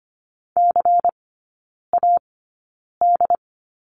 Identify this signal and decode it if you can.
Morse code, heard as a keyed tone, about 25 wpm, spaced as for 9 wpm, 700 Hz. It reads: CAD
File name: course / 25:10wpm / 093 Lesson 16e - Letter c - Three Characters 25wpm